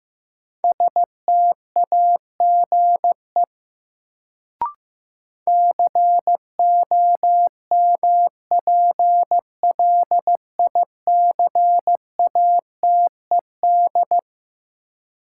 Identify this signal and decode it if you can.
Morse code, heard as a keyed tone, about 15 wpm, 700 Hz. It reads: STAGE COMPLICATED
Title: Sentences from Top 1200 Words - New Word Drill - 15wpm